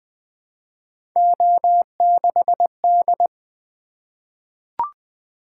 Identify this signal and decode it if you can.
Morse code, heard as a keyed tone, about 20 wpm, 700 Hz. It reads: O6D
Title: Three Letter-Number Combinations - 20wpm